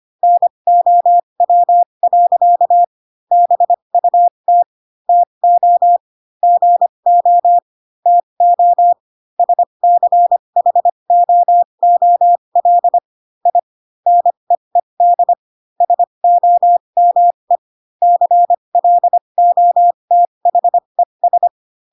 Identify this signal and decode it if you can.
Morse code, heard as a keyed tone, about 25 wpm, spaced as for 22 wpm, 700 Hz. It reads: NOW. BUT TO GO TO SCHOOL I NEED SOME CLOTHES